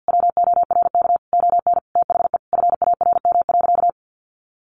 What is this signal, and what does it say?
Morse code, heard as a keyed tone, about 50 wpm, 700 Hz. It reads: WORK ON THE FARM.
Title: Sentences from Top 1200 Words 50wpm